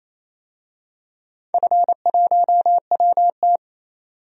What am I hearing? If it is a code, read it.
Morse code, heard as a keyed tone, about 27 wpm, 690 Hz. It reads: F1WT
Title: DXCC-100 Call Signs - 4 Characters - 28wpm